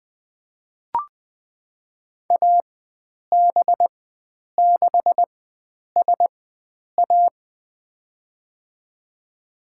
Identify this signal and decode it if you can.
Morse code, heard as a keyed tone, about 20 wpm, spaced as for 9 wpm, 700 Hz. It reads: AB6SA
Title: USA Call Signs - 5 Characters - 4x Character Spacing - 20wpm